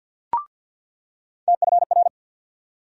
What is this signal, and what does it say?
Morse code, heard as a keyed tone, about 47 wpm, 690 Hz. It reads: TFR
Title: USA Call Sign Postfixes - 3 Letters 50wpm